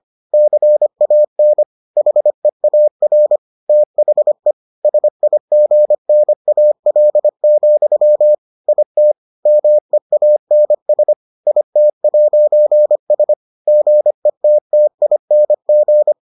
Morse code, heard as a keyed tone, about 25 wpm, 600 Hz. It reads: CAN HEAR THE SIGNAL, IT MEANS IT'S GETTING